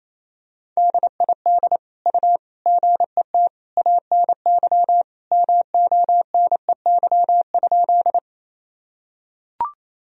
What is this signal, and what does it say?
Morse code, heard as a keyed tone, about 28 wpm, 700 Hz. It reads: DID U GET ANY MONEY?
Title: Sentences from Top 400 Words and Abbreviations 28wpm